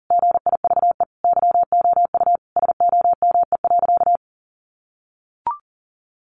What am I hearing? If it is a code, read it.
Morse code, heard as a keyed tone, about 40 wpm, 700 Hz. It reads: GIVE YOU SOME.
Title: Sentences from Top 800 Words 40wpm